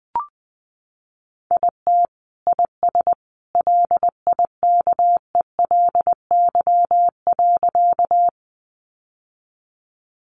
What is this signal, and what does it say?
Morse code, heard as a keyed tone, about 20 wpm, 700 Hz. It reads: IT IS LIKELY.
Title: Sentences from Top 600 Words 20wpm